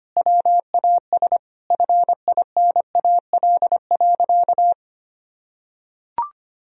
Morse code, heard as a keyed tone, about 25 wpm, 700 Hz. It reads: WAS FINAL.
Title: Sentences from Top 1000 Words - Review Part 2 - 25wpm